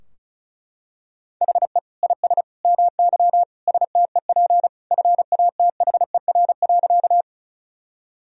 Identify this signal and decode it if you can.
Morse code, heard as a keyed tone, about 34 wpm, 700 Hz. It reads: HE IS MY STEP FATHER.
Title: Sentences from Top 1000 Words - Review Part 2 - 35wpm